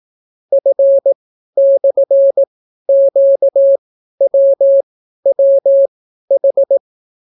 Morse code, heard as a keyed tone, about 18 wpm, spaced as for 12 wpm, 550 Hz. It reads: F/QWWH